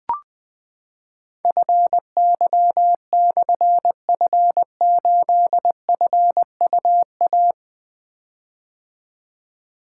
Morse code, heard as a keyed tone, about 20 wpm, 700 Hz. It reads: FY/F8FUA
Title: Difficult Call Signs 20wpm